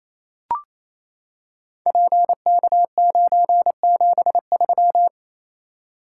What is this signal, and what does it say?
Morse code, heard as a keyed tone, about 27 wpm, 700 Hz. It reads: PK973
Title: Code Groups - 5 Letter:Numbers - 28wpm